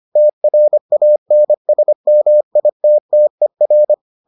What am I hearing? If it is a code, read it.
Morse code, heard as a keyed tone, about 25 wpm, 600 Hz. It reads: TRANSMITTER